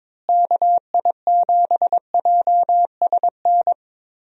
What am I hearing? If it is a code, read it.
Morse code, heard as a keyed tone, about 22 wpm, 700 Hz. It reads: KI7JSN
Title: USA Call Signs - 6 Characters - Repeated 3x - 22wpm